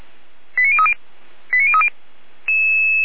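Five-tone selective call (ZVEI), heard as quick runs of five tones, tones of about 70 ms. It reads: 89030 89030
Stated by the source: ZVEI five-tone call 89140 received on SDR, fire brigade Orlamünde (feuerwehr-orlamuende).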